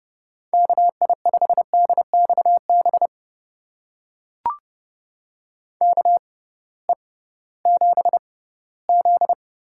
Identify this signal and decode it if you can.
Morse code, heard as a keyed tone, about 30 wpm, spaced as for 17 wpm, 700 Hz. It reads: KI5DXB KE7Z
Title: USA Call Signs - 6 Characters - 6x Character Spacing - 30wpm